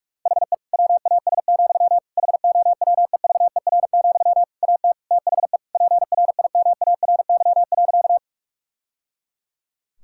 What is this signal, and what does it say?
Morse code, heard as a keyed tone, about 46 wpm, 700 Hz. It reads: HE WAS, HOWEVER, AT THE PRIMARY.
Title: Sentences from Top 1000 Words - Review Part 5 - 45wpm